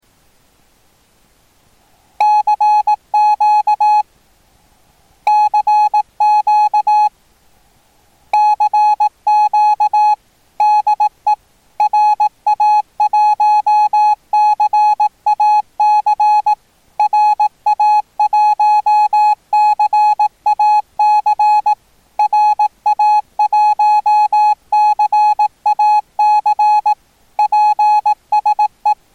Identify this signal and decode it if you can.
Morse code, heard as a keyed tone, about 18 wpm, 840 Hz. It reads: CQ CQ CQ DE RA1CAC RA1CAC RA1CAC PSE
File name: attachments / 1cq noagc 2cq agcin 3cq agcout